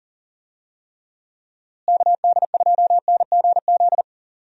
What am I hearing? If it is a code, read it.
Morse code, heard as a keyed tone, about 38 wpm, 700 Hz. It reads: KD2NGZ
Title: USA Call Signs - 6 Characters - Repeated 2x - 40wpm